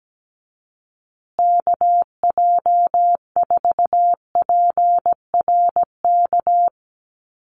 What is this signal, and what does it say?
Morse code, heard as a keyed tone, about 17 wpm, 700 Hz. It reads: KJ4PRK
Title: USA Call Signs Repeated 3x 17wpm